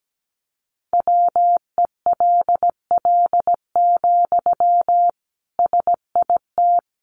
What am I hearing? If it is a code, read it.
Morse code, heard as a keyed tone, about 17 wpm, 700 Hz. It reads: WELL, SIT
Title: Sentences from Top 500 Words 17wpm